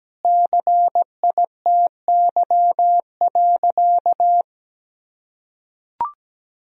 Morse code, heard as a keyed tone, about 17 wpm, 700 Hz. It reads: CITY.
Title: Sentences from Top 1000 Words - Review Part 1 - 17wpm